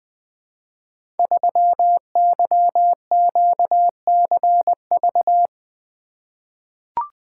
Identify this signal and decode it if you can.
Morse code, heard as a keyed tone, about 20 wpm, 700 Hz. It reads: 3YQCV
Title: Code Groups - 5 Letter:Numbers - 20wpm